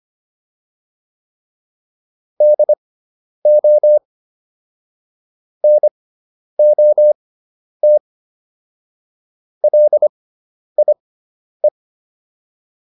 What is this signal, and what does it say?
Morse code, heard as a keyed tone, about 25 wpm, spaced as for 10 wpm, 600 Hz. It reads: DO NOT LIE